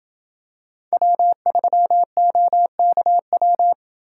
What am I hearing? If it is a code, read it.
Morse code, heard as a keyed tone, about 27 wpm, 700 Hz. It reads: W3OKW